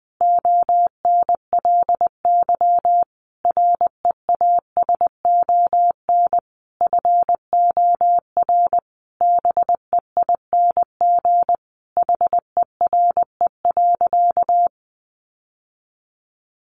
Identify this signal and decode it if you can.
Morse code, heard as a keyed tone, about 20 wpm, 700 Hz. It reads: ONLY REASON FOR BEING HERE.
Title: Sentences from Top 700 Words 20wpm